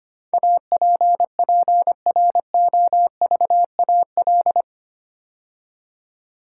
Morse code, heard as a keyed tone, about 25 wpm, 700 Hz. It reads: APPROVAL
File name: 8 Letter Words - First 3 Letters Given - 25wpm